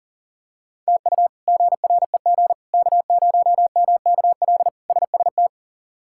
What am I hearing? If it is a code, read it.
Morse code, heard as a keyed tone, about 40 wpm, 700 Hz. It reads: TU GREG K0MKL SST
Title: K1USN SST QSO Practice - 40wpm